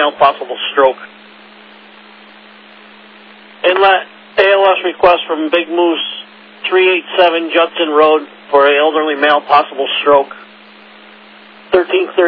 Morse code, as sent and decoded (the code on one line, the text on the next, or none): none